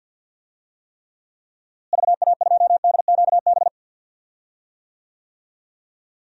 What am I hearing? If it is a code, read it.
Morse code, heard as a keyed tone, about 47 wpm, 700 Hz. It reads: VA2DQB
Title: DXCC-100 Call Signs - 6 Characters - Repeated 3x - 50wpm